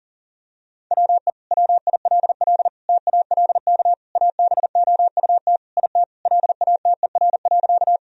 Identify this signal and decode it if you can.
Morse code, heard as a keyed tone, about 40 wpm, 700 Hz. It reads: WE WILL TALK ABOUT IT LATER.
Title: Sentences from Top 1000 Words - Review Part 4 - 40wpm